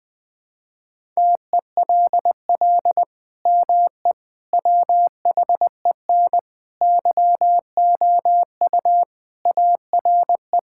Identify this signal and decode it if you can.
Morse code, heard as a keyed tone, about 20 wpm, 700 Hz. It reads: TELL ME WHEN YOU ARE
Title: Sentences with 2-Word Phrases - 20wpm